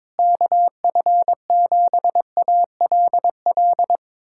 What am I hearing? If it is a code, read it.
Morse code, heard as a keyed tone, about 22 wpm, 700 Hz. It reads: KF7ALL
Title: USA Callsign DE Callsign 22wpm